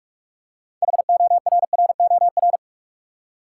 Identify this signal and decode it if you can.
Morse code, heard as a keyed tone, about 46 wpm, 700 Hz. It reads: HORROR
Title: Six Letter Words - 45wpm